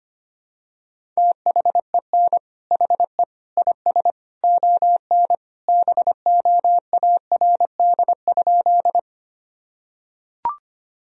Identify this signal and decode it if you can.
Morse code, heard as a keyed tone, about 25 wpm, 700 Hz. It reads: THEN HE IS ON BOARD?
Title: Sentences from Top 1000 Words - Review Part 4 - 25wpm